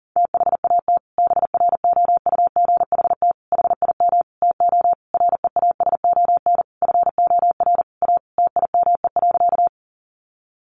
Morse code, heard as a keyed tone, about 37 wpm, 700 Hz. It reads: THAT BROUGHT HIM TO REASON FOR A TIME.
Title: Sentences from Top 700 Words 40wpm